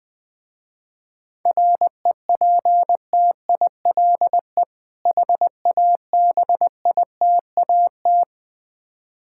Morse code, heard as a keyed tone, about 21 wpm, 700 Hz. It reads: REPTILE HABITAT
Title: Difficult Words with Hints - 20wpm